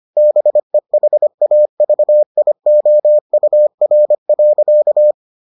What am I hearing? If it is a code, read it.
Morse code, heard as a keyed tone, about 25 wpm, 600 Hz. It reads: BEHAVIOUR.